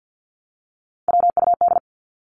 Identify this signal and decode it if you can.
Morse code, heard as a keyed tone, about 48 wpm, 700 Hz. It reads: WUD
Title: Abbreviations Repeated 3x - 50wpm